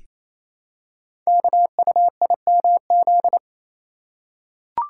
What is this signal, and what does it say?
Morse code, heard as a keyed tone, about 29 wpm, 700 Hz. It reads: KUIMZ
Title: Code Groups - 5 Letters - 28wpm